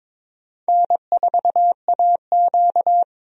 Morse code, heard as a keyed tone, about 22 wpm, 700 Hz. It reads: N4AQ
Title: USA Call Signs - 4 Characters - 22wpm